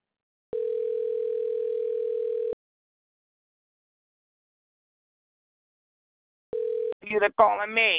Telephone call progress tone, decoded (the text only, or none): ringback tone